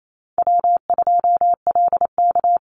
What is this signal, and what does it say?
Morse code, heard as a keyed tone, about 28 wpm, 700 Hz. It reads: W2LK
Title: Easy Call Signs - 28wpm